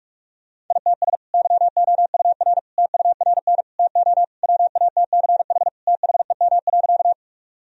Morse code, heard as a keyed tone, about 47 wpm, 700 Hz. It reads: ITS YOUR TURN TO WATCH THEM.
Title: Sentences from Top 1000 Words - Review Part 1 - 45wpm